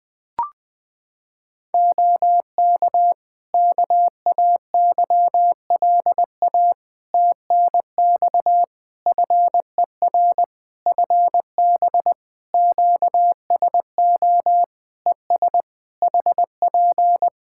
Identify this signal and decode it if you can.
Morse code, heard as a keyed tone, about 20 wpm, 700 Hz. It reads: OK KAYLA TNX FER FB QSO ES HP
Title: K1USN SST QSO Practice 20wpm